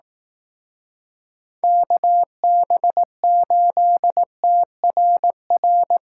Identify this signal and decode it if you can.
Morse code, heard as a keyed tone, about 18 wpm, 700 Hz. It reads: KB8TRR
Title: Speed Racing - USA 2x3 Call Signs 15wpm